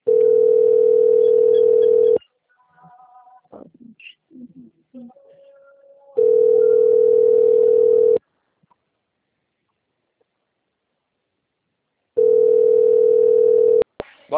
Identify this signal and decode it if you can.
Ringback tone (North America), 440 + 480 Hz.